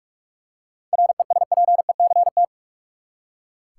Morse code, heard as a keyed tone, about 42 wpm, 700 Hz. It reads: RESPECT
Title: Seven Letter Words - 45wpm